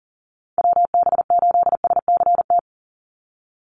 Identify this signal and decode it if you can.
Morse code, heard as a keyed tone, about 40 wpm, 700 Hz. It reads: WB8SCT
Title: Call Signs - 40wpm